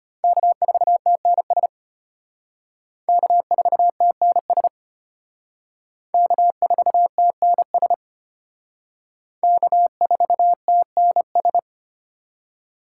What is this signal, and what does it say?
Morse code, heard as a keyed tone, about 31 wpm, 700 Hz. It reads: K4TNS K4TNS K4TNS K4TNS
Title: Speed Racing - USA 1x3 Call Signs 25wpm